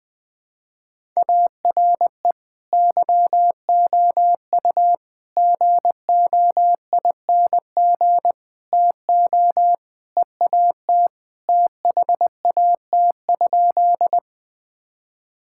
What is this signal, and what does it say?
Morse code, heard as a keyed tone, about 20 wpm, 700 Hz. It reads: ARE YOU GOING TO EAT THAT?
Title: Sentences with 4-Word Phrases - 20wpm